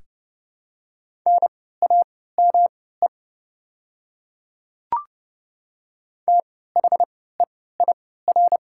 Morse code, heard as a keyed tone, about 29 wpm, spaced as for 17 wpm, 700 Hz. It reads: NAME THEIR
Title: 3x Character Spacing - Top 200 Words - 30wpm